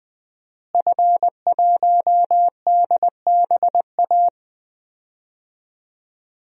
Morse code, heard as a keyed tone, about 20 wpm, 700 Hz. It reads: F1DBA
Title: DXCC-100 Call Signs - 5 Characters - Repeated 3x - 20wpm